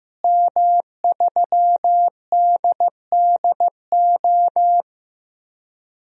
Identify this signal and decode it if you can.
Morse code, heard as a keyed tone, about 15 wpm, 700 Hz. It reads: M3DDO
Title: DXCC-100 Call Signs - 5 Characters - Repeated 3x - 15wpm